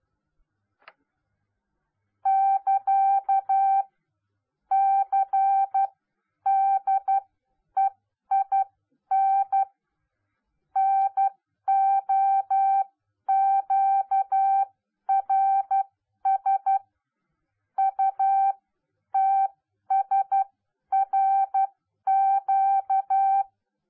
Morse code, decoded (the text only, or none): <KA> CDEIN NOQRS UTSRQ